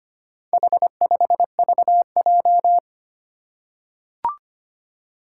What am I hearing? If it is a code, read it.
Morse code, heard as a keyed tone, about 25 wpm, 700 Hz. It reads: H5VJ
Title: Four Challenging Letter-Number Combinations 25wpm